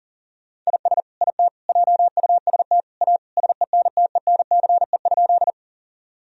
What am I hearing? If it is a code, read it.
Morse code, heard as a keyed tone, about 40 wpm, 700 Hz. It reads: IS IT JUST A SENTENCE?
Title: Sentences from Top 1000 Words - Review Part 1 - 40wpm